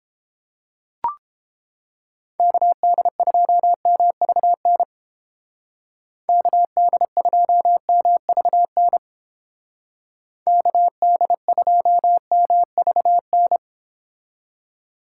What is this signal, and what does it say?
Morse code, heard as a keyed tone, about 29 wpm, 700 Hz. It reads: KD2MVN KD2MVN KD2MVN